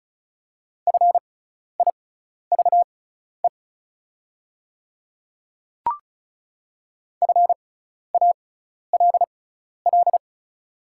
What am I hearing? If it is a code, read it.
Morse code, heard as a keyed tone, about 36 wpm, spaced as for 12 wpm, 700 Hz. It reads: FIVE FALL
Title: 6x Character Spacing - Top 500 Words - 35wpm